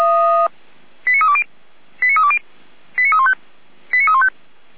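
Five-tone selective call (ZVEI), heard as quick runs of five tones, tones of about 70 ms. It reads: 89320 89320 89315 89315
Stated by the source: ZVEI five-tone call 89265 received by SDR.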